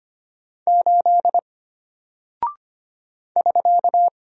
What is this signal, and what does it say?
Morse code, heard as a keyed tone, about 25 wpm, 700 Hz. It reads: 8 <SK>
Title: Single Character (Letter:Number:Punctuation:Prosign) Alt 25wpm